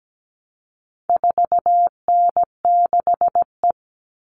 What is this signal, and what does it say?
Morse code, heard as a keyed tone, about 17 wpm, 700 Hz. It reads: 4N6E
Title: All Cut Numbers - 4-Digit 17wpm